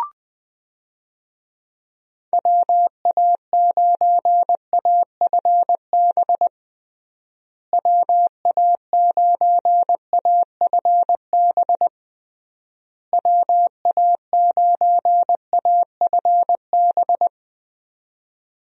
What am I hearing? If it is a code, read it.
Morse code, heard as a keyed tone, about 20 wpm, 700 Hz. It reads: WA9AFB WA9AFB WA9AFB
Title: USA Call Signs - 6 Characters - Repeated 3x - 20wpm